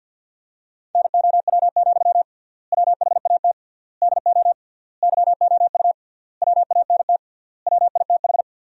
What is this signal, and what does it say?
Morse code, heard as a keyed tone, about 51 wpm, 700 Hz. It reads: NOW, WHAT DO YOU WANT WITH
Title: 3x Word-Spacing - Sentences from Top 200 Words - 50wpm